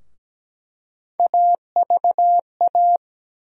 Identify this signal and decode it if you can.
Morse code, heard as a keyed tone, about 17 wpm, 700 Hz. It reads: AVA
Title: USA Call Sign Postfixes - 3 Letters 17wpm